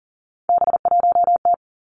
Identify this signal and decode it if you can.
Morse code, heard as a keyed tone, about 38 wpm, 700 Hz. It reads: B1T
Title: Three Challenging Letter-Number Combinations 40wpm